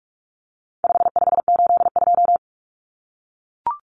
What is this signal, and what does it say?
Morse code, heard as a keyed tone, about 44 wpm, 700 Hz. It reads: 5582
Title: Numbers - 4-Digit 45wpm